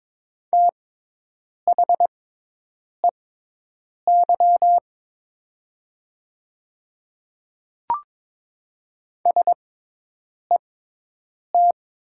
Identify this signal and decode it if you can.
Morse code, heard as a keyed tone, about 22 wpm, spaced as for 8 wpm, 700 Hz. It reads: THEY SET
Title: 6x Character Spacing - Top 300 Words - 22wpm